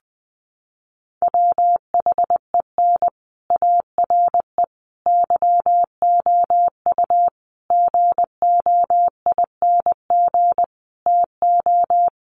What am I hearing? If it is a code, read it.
Morse code, heard as a keyed tone, about 20 wpm, 700 Hz. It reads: WHEN ARE YOU GOING TO
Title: Sets of 5 Words Encore2 20wpm